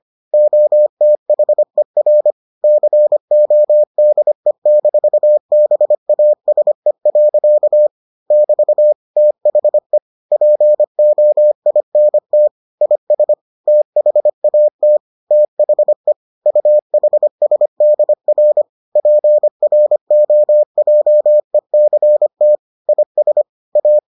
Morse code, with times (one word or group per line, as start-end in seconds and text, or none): 0.3-2.3 OTHER
2.6-7.9 CODE-BASE.
8.3-8.9 <BT>
9.2-10.0 THE
10.3-12.5 POINT
12.8-13.3 IS
13.7-15.0 THAT
15.3-16.1 THE
16.5-18.6 UHSDR
19.0-22.6 PROJECT
22.9-23.4 IS
23.8-24.0 A